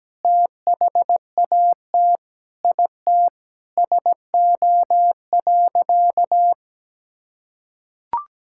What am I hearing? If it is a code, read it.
Morse code, heard as a keyed tone, about 17 wpm, 700 Hz. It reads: THAT IT SO.
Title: Sentences from Top 1600 Words 17wpm